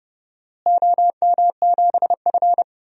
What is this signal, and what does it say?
Morse code, heard as a keyed tone, about 31 wpm, 700 Hz. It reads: OM7F
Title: DXCC-100 Call Signs - 4 Characters - Repeated 2x - 30wpm